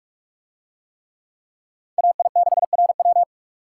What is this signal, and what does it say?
Morse code, heard as a keyed tone, about 45 wpm, 700 Hz. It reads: AI6RW